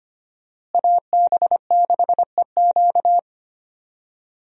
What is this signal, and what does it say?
Morse code, heard as a keyed tone, about 25 wpm, 700 Hz. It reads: AB6EQ
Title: USA Call Signs - 5 Characters - Repeated 3x - 25wpm